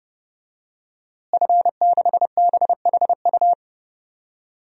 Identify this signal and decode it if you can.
Morse code, heard as a keyed tone, about 30 wpm, 700 Hz. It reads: F6BHU
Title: DXCC-100 Call Signs - 5 Characters - Repeated 2x - 30wpm